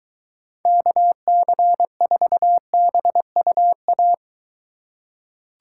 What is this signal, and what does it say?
Morse code, heard as a keyed tone, about 21 wpm, 700 Hz. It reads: KC4BUA